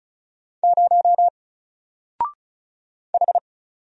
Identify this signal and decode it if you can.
Morse code, heard as a keyed tone, about 36 wpm, 700 Hz. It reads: 0 H
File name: Single-Letter-Number-35wpm